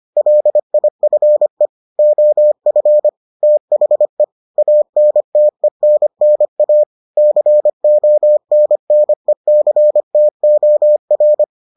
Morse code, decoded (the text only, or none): LIFE OF THE ANTENNA CONNECTOR